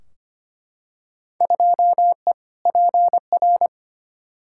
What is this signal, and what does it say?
Morse code, heard as a keyed tone, about 25 wpm, 700 Hz. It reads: 2E PR
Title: Field Day Exchanges Encore - 25wpm